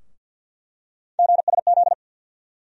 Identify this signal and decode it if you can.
Morse code, heard as a keyed tone, about 51 wpm, 700 Hz. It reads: GSZ
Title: USA Call Sign Postfixes - 3 Letters 50wpm